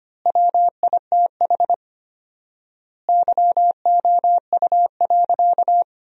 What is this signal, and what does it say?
Morse code, heard as a keyed tone, about 25 wpm, 700 Hz. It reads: WITH YOU.
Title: 4x Word-Spacing - Sentences from Top 300 Words - 25wpm